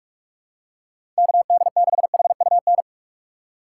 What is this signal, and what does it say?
Morse code, heard as a keyed tone, about 46 wpm, 700 Hz. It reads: KD6HUN
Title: USA Call Signs - 6 Characters - 6x Character Spacing - 45wpm